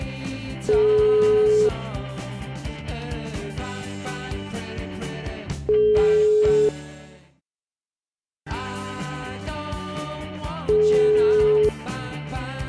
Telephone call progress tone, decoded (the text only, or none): ringback tone